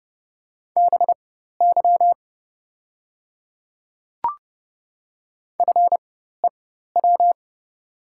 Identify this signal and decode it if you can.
Morse code, heard as a keyed tone, about 29 wpm, spaced as for 14 wpm, 700 Hz. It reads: BY FEW
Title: 4x Character Spacing - Top 300 Words - 30wpm